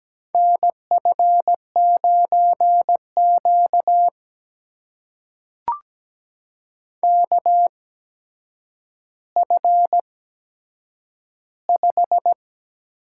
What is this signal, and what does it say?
Morse code, heard as a keyed tone, about 17 wpm, spaced as for 8 wpm, 700 Hz. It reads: NF9Q KF5
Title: USA Call Signs - 8x Character Spacing - 17wpm